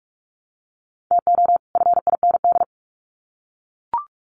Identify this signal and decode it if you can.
Morse code, heard as a keyed tone, about 44 wpm, 700 Hz. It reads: TO FIND
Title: Sets of 2 Words 45wpm